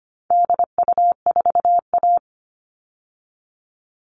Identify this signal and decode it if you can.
Morse code, heard as a keyed tone, about 25 wpm, 700 Hz. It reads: DU4A
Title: All Cut Numbers - 4-Digit 25wpm